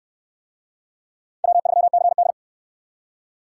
Morse code, heard as a keyed tone, about 65 wpm, 700 Hz. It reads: W3CD